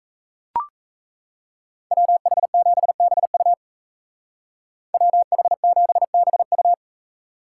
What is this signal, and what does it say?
Morse code, heard as a keyed tone, about 33 wpm, 700 Hz. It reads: WH7BU WH7BU
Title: Speed Racing - USA Call Signs 28wpm